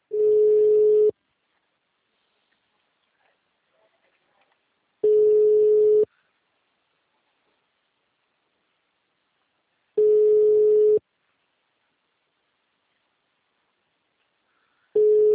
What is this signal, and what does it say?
Ringback tone (Europe, 425 Hz).